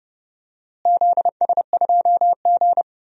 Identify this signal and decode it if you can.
Morse code, heard as a keyed tone, about 30 wpm, 700 Hz. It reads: ZS2G
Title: DXCC-100 Call Signs - 4 Characters - Repeated 2x - 30wpm